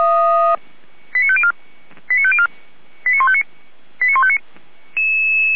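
Five-tone selective call (ZVEI), heard as quick runs of five tones, tones of about 70 ms. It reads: 89504 89504 89159 89159
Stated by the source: ZVEI five-tone selective call 89170 received by SDR.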